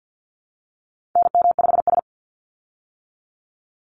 Morse code, heard as a keyed tone, about 50 wpm, 700 Hz. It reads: NM5S
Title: Call Signs - 50wpm